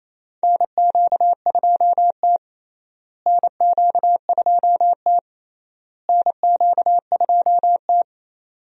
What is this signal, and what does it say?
Morse code, heard as a keyed tone, about 28 wpm, 700 Hz. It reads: NQ2T NQ2T NQ2T
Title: USA Call Signs - 4 Characters - Repeated 3x - 28wpm